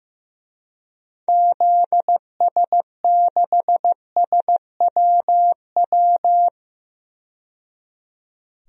Morse code, heard as a keyed tone, about 15 wpm, 700 Hz. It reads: ZS6SWW
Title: DXCC-100 Call Signs - 6 Characters - 15wpm